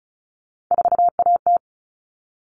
Morse code, heard as a keyed tone, about 34 wpm, 700 Hz. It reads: 4AT
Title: All Cut Numbers - 3-Digit 35wpm